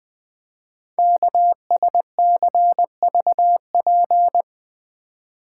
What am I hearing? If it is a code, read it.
Morse code, heard as a keyed tone, about 20 wpm, 700 Hz. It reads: KSCVP
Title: Code Groups - 5 Letters - 20wpm